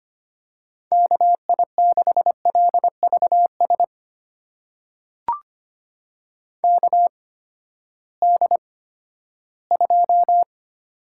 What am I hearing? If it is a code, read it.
Morse code, heard as a keyed tone, about 25 wpm, spaced as for 14 wpm, 700 Hz. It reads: KI6LVS KD2
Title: USA Call Signs - 6 Characters - 8x Character Spacing - 25wpm